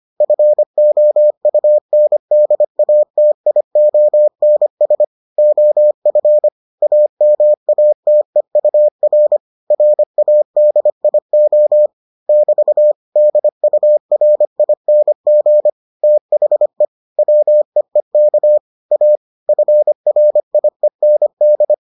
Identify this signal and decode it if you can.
Morse code, heard as a keyed tone, about 25 wpm, 600 Hz. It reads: FOUNDATIONS OF AMATEUR RADIO <BT> DURING THE WEEK A FRIEND